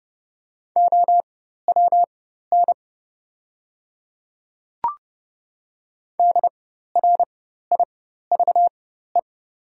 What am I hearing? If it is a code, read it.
Morse code, heard as a keyed tone, about 30 wpm, spaced as for 14 wpm, 700 Hz. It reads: OWN DRIVE
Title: 4x Character Spacing - Top 500 Words - 30wpm